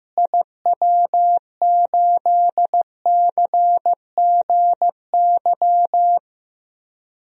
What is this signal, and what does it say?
Morse code, heard as a keyed tone, about 16 wpm, 700 Hz. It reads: IW8CGY
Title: DXCC-100 Call Signs - 6 Characters - Repeated 3x - 15wpm